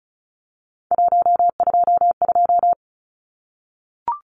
Morse code, heard as a keyed tone, about 34 wpm, 700 Hz. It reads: 122
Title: Numbers - 3-Digit 35wpm